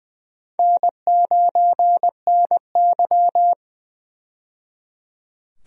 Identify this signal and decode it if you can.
Morse code, heard as a keyed tone, about 20 wpm, 700 Hz. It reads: N9NY
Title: USA Call Signs - 4 Characters - 20wpm